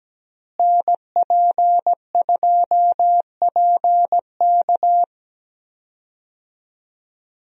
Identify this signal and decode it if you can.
Morse code, heard as a keyed tone, about 17 wpm, 700 Hz. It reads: NP2PK
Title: USA Call Signs - 5 Characters - Repeated 2x - 17wpm